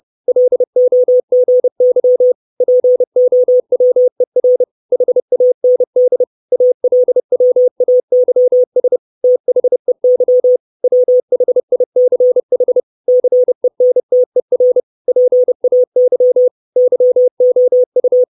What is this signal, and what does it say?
Morse code, heard as a keyed tone, about 30 wpm, 500 Hz. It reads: LOGY POWER HAND ALWAYS THEY WHICH CENTER PAY YOU
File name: 632 Words 30WPM